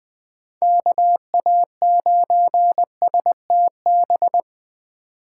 Morse code, heard as a keyed tone, about 20 wpm, 700 Hz. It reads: KA9STB
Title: USA Call Signs - 6 Characters - 2x Character Spacing - 20wpm